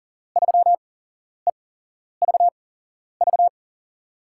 Morse code, heard as a keyed tone, about 40 wpm, spaced as for 11 wpm, 700 Hz. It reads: 3EVV
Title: USA Call Signs - 5 Characters - 8x Character Spacing - 40wpm